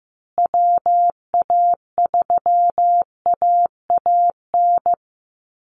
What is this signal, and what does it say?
Morse code, heard as a keyed tone, about 15 wpm, 700 Hz. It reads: WA3AAN
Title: Call Signs Repeated 2x - 15wpm